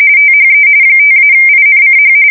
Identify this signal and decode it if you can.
RTTY, about 45.45 baud. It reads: THE 5-INCH
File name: fsk170 / battleshipnj